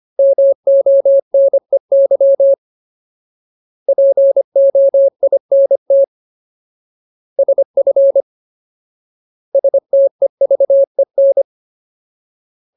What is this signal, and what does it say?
Morse code, heard as a keyed tone, about 25 wpm, 550 Hz. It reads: MONEY POINT SF STEVEN
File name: Ultimate Word List / 25WPM Ultimate